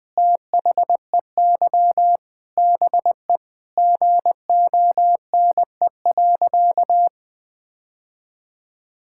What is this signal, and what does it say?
Morse code, heard as a keyed tone, about 20 wpm, 700 Hz. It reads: THEY BE GONE.
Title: Sentences with 4-Word Phrases - 20wpm